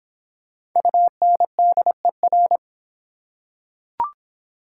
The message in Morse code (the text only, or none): UNDER